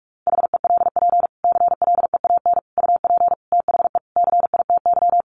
Morse code, heard as a keyed tone, about 44 wpm, 700 Hz. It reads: HELP CLEAN UP THE CITY
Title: Sentences from Top 1100 Words 45wpm